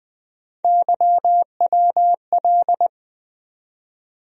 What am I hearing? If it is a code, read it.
Morse code, heard as a keyed tone, about 20 wpm, 700 Hz. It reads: YWL
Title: USA Call Sign Postfixes - 3 Letters 20wpm